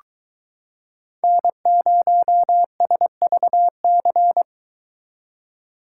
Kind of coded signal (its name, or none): Morse code